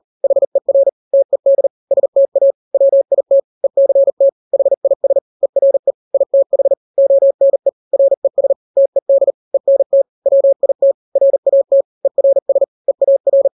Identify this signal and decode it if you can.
Morse code, heard as a keyed tone, about 40 wpm, 550 Hz. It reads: HER TED STA WIT ECT HIS ERE ITH ONE RES TED ENT WIT RAT ERS EAR
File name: n-grams / 3-grams 40 35 WPM